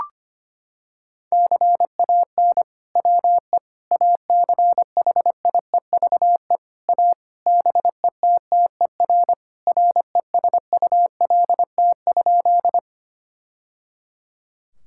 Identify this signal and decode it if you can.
Morse code, heard as a keyed tone, about 25 wpm, 700 Hz. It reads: CAN WE ACHIEVE A BETTER RESULT?
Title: Sentences from Top 1000 Words - Review Part 6 - 25wpm